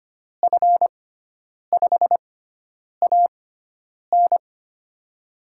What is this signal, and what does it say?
Morse code, heard as a keyed tone, about 25 wpm, spaced as for 9 wpm, 700 Hz. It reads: F5AN
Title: USA Call Signs - 5 Characters - 6x Character Spacing - 25wpm